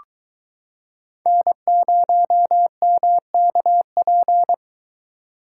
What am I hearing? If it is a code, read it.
Morse code, heard as a keyed tone, about 23 wpm, 700 Hz. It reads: N0MKP